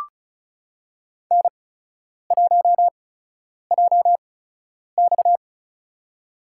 Morse code, heard as a keyed tone, about 34 wpm, spaced as for 10 wpm, 700 Hz. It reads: N1JX
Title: USA Call Signs - 4 Characters - 8x Character Spacing - 35wpm